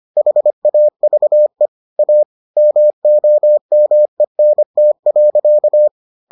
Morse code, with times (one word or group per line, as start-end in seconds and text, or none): 0.2-1.7 HAVE
2.0-2.2 A
2.6-5.9 MOMENT.